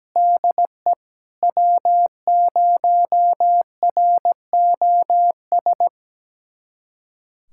Morse code, heard as a keyed tone, about 17 wpm, 700 Hz. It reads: DE W0ROS
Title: USA Callsign DE Callsign - 5 Characters - 17wpm